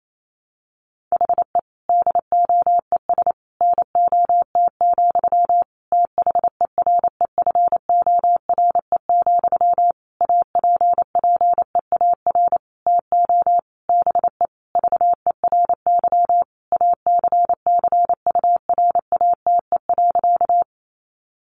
Morse code, heard as a keyed tone, about 28 wpm, 700 Hz. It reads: HE DOES NOT, THEREFORE, APPEAR TO BE VERY ACCURATE.
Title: Sentences from Top 1200 Words 28wpm